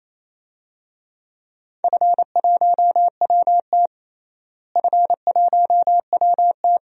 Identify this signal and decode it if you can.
Morse code, heard as a keyed tone, about 28 wpm, 700 Hz. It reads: F1WT F1WT
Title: DXCC-100 Call Signs - 4 Characters - Repeated 3x - 28wpm